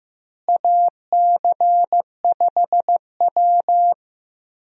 Morse code, heard as a keyed tone, about 15 wpm, 700 Hz. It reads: AC5W
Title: USA Call Signs - 4 Characters - Repeated 2x - 15wpm